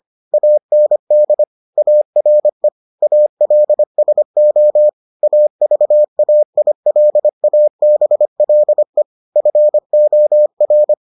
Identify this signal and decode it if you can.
Morse code, heard as a keyed tone, about 25 wpm, 600 Hz. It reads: AND ARE ALSO AVAILABLE FOR